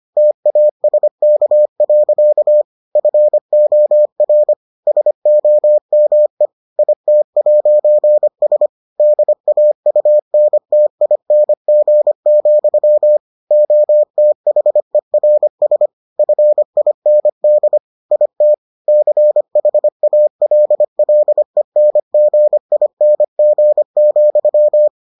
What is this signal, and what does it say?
Morse code, heard as a keyed tone, about 25 wpm, 600 Hz. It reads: TASK. FOR SOME IT'S DAUNTING, OTHERS FIND IT CHALLENGING,